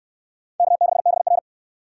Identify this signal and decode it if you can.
Morse code, heard as a keyed tone, about 69 wpm, 700 Hz. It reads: K6BR